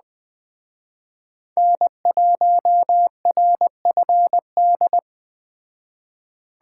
Morse code, heard as a keyed tone, about 20 wpm, 700 Hz. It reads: N1RFD